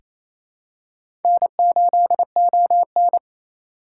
Morse code, heard as a keyed tone, about 28 wpm, 700 Hz. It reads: N8ON
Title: USA Call Signs - 4 Characters - 6x Character Spacing - 28wpm